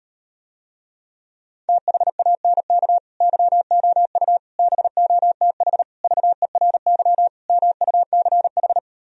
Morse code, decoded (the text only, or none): THANK YOU BOTH VERY MUCH